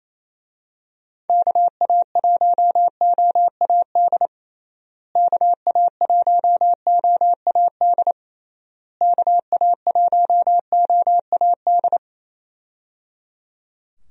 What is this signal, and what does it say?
Morse code, heard as a keyed tone, about 28 wpm, 700 Hz. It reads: KA1OAD KA1OAD KA1OAD
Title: USA Call Signs - 6 Characters - Repeated 3x - 28wpm